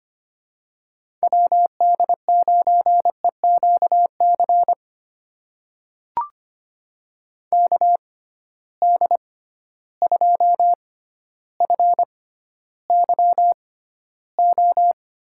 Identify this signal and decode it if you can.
Morse code, heard as a keyed tone, about 25 wpm, spaced as for 13 wpm, 700 Hz. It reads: WD9EQC KD2FYO
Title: USA Call Signs - 6 Characters - 6x Character Spacing - 25wpm